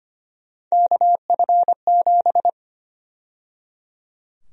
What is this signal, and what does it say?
Morse code, heard as a keyed tone, about 25 wpm, 700 Hz. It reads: KF7
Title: USA Call Sign Prefixes - 25wpm